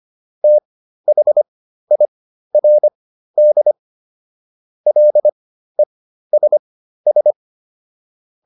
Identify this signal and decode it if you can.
Morse code, heard as a keyed tone, about 25 wpm, spaced as for 13 wpm, 600 Hz. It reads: THIRD LESS